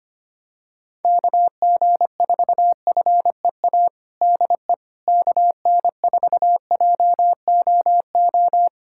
Morse code, heard as a keyed tone, about 25 wpm, 700 Hz. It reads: KG4FEA DE KN4JOO
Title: USA Callsign DE Callsign - 6 Characters - 25wpm